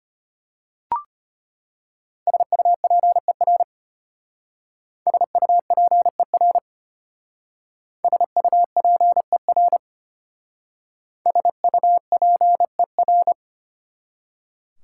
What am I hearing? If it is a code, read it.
Morse code, heard as a keyed tone, about 31 wpm, 700 Hz. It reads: SUPER SUPER SUPER SUPER
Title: Speed-Racing Prefixes and Suffixes 25wpm